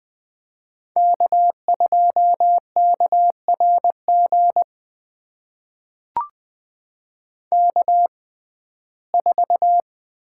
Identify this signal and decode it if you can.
Morse code, heard as a keyed tone, about 20 wpm, spaced as for 14 wpm, 700 Hz. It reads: K2KRG K4
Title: USA Call Signs - 5 Characters - 6x Character Spacing - 20wpm